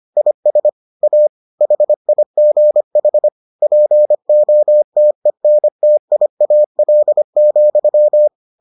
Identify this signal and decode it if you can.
Morse code, heard as a keyed tone, about 25 wpm, 600 Hz. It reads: IS A HIGH POTENTIAL,